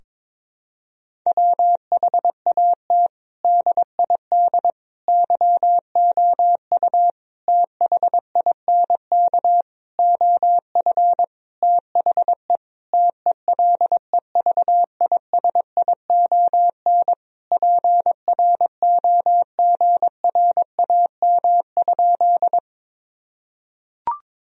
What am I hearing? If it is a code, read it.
Morse code, heard as a keyed tone, about 22 wpm, 700 Hz. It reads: WHAT DID YOU THINK OF THE TELEVISION PROGRAM?
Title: Sentences from Top 1000 Words - Review Part 4 - 22wpm